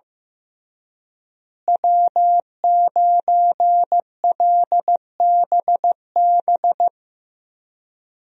Morse code, heard as a keyed tone, about 15 wpm, 700 Hz. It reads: W9LBB
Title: USA Call Signs - 5 Characters - Repeated 3x - 15wpm